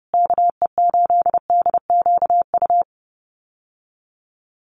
Morse code, heard as a keyed tone, about 30 wpm, 700 Hz. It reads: KE8DQU
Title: USA Call Signs Repeated 3x 30wpm